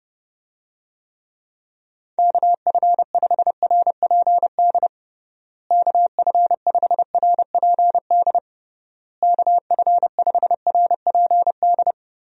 Morse code, heard as a keyed tone, about 30 wpm, 700 Hz. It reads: KF5RPD KF5RPD KF5RPD
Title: USA Call Signs - 6 Characters - Repeated 3x - 30wpm